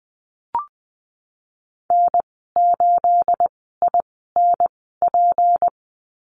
Morse code, heard as a keyed tone, about 20 wpm, spaced as for 14 wpm, 700 Hz. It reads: N8INP
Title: 231 Lesson 37g - Letter x - Call Signs 20wpm